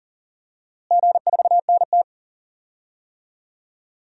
Morse code, heard as a keyed tone, about 40 wpm, 700 Hz. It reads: G4NT
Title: DXCC-100 Call Signs - 4 Characters - Repeated 2x - 40wpm